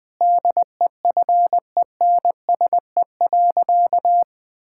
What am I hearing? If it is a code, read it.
Morse code, heard as a keyed tone, about 20 wpm, 700 Hz. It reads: DEFENSE.
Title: Difficult Sentences with Hints - 20wpm